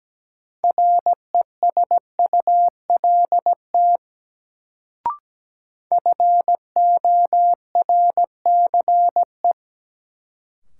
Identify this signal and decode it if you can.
Morse code, heard as a keyed tone, about 17 wpm, 700 Hz. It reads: RESULT FORCE
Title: Sentences from Top 700 Words - New Word Drill - 17wpm